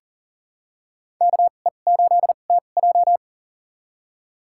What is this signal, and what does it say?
Morse code, heard as a keyed tone, about 38 wpm, spaced as for 28 wpm, 700 Hz. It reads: KE8TJ
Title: USA Call Signs - 5 Characters - 2x Character Spacing - 40wpm